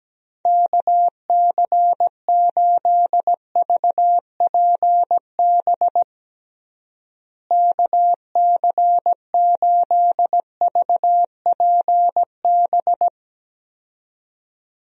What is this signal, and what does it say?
Morse code, heard as a keyed tone, about 17 wpm, 700 Hz. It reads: KC8VPB KC8VPB
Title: USA Call Signs - 6 Characters - Repeated 2x - 17wpm